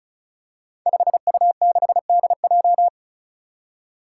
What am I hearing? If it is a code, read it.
Morse code, heard as a keyed tone, about 36 wpm, 700 Hz. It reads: 5U6DJ
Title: Five Challenging Letter-Number Combinations 35wpm